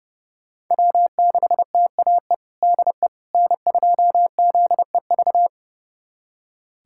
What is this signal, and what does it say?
Morse code, heard as a keyed tone, about 30 wpm, 700 Hz. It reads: W6TAE DE N2ZEV
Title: USA Callsign DE Callsign - 5 Characters - 30wpm